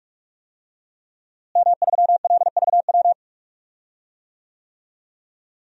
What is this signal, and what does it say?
Morse code, heard as a keyed tone, about 41 wpm, 700 Hz. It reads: M3LVW
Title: DXCC-100 Call Signs - 5 Characters - Repeated 3x - 45wpm